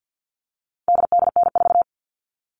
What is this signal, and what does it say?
Morse code, heard as a keyed tone, about 50 wpm, 700 Hz. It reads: DDN4
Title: All Cut Numbers - 4-Digit 50wpm